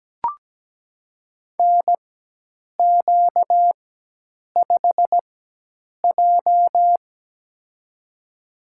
Morse code, heard as a keyed tone, about 17 wpm, spaced as for 8 wpm, 700 Hz. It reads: NQ5J
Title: USA Call Signs - 4 Characters - 4x Character Spacing - 17wpm